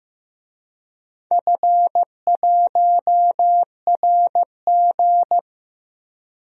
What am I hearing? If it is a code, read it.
Morse code, heard as a keyed tone, about 15 wpm, 700 Hz. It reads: F1RG
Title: DXCC-100 Call Signs - 4 Characters - 15wpm